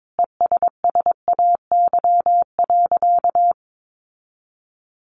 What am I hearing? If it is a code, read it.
Morse code, heard as a keyed tone, about 22 wpm, 700 Hz. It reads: ESSAY.
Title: Sentences from Top 1500 Words 22wpm